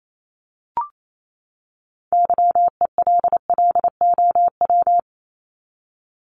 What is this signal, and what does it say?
Morse code, heard as a keyed tone, about 28 wpm, 700 Hz. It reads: YELLOW
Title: Top 1751-2000 Words 28wpm